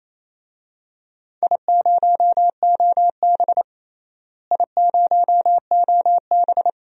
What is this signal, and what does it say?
Morse code, heard as a keyed tone, about 28 wpm, 700 Hz. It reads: I0OB I0OB
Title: DXCC-100 Call Signs - 4 Characters - Repeated 3x - 28wpm